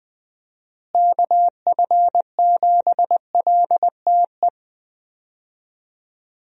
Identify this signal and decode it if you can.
Morse code, heard as a keyed tone, about 21 wpm, 700 Hz. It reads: KF7LTE
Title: USA Call Signs - 6 Characters - Repeated 2x - 20wpm